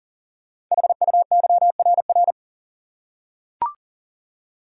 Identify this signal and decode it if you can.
Morse code, heard as a keyed tone, about 42 wpm, 700 Hz. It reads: HUYRR
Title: Code Groups - 5 Letters - 40wpm